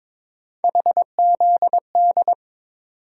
Morse code, heard as a keyed tone, about 22 wpm, 700 Hz. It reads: HZD
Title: USA Call Sign Postfixes - 3 Letters 22wpm